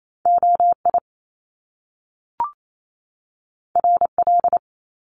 Morse code, heard as a keyed tone, about 28 wpm, 700 Hz. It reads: OI RL
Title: Top Two Letter Combinations in Words 28wpm